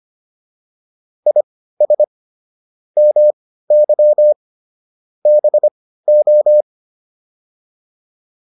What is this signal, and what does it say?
Morse code, heard as a keyed tone, about 25 wpm, spaced as for 15 wpm, 600 Hz. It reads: IS MY BO